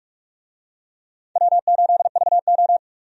Morse code, heard as a keyed tone, about 46 wpm, 700 Hz. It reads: W8VO
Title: USA Call Signs - 4 Characters - Repeated 2x - 45wpm